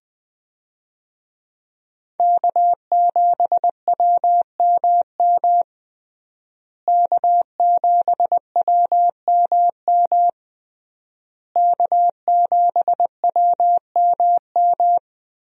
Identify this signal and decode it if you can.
Morse code, heard as a keyed tone, about 20 wpm, 700 Hz. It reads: K7WMM K7WMM K7WMM
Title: USA Call Signs - 5 Characters - Repeated 3x - 20wpm